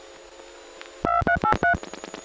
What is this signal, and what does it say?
DTMF, heard as telephone touch tones, tones of about 120 ms. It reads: 2383